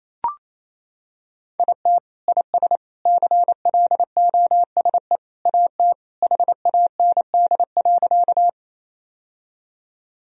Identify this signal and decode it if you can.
Morse code, heard as a keyed tone, about 28 wpm, 700 Hz. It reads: IT IS CLOSE AT HAND.
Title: Sentences from Top 1000 Words - Review Part 1 - 28wpm